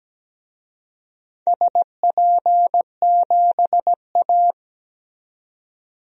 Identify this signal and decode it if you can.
Morse code, heard as a keyed tone, about 17 wpm, 700 Hz. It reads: SP7A
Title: DXCC-100 Call Signs - 4 Characters - 17wpm